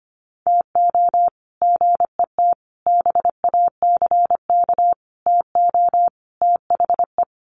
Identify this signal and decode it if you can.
Morse code, heard as a keyed tone, about 25 wpm, 700 Hz. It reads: TO GET BACK TO THE
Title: Sets of 5 Words Encore 25wpm